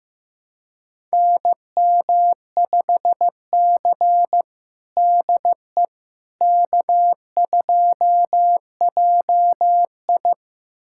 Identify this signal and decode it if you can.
Morse code, heard as a keyed tone, about 15 wpm, 700 Hz. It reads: NM5C DE K2JI
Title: USA Callsign DE Callsign - 4 Characters - 15wpm